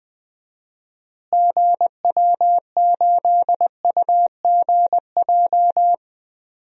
Morse code, heard as a keyed tone, about 20 wpm, 700 Hz. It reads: GW8UGJ
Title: DXCC-100 Call Signs - 6 Characters - Repeated 3x - 20wpm